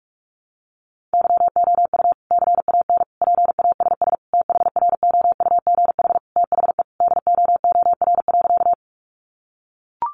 Morse code, heard as a keyed tone, about 44 wpm, 700 Hz. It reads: YOU CAN PASS THROUGH THE DOOR.
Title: Sentences from Top 400 Words 45wpm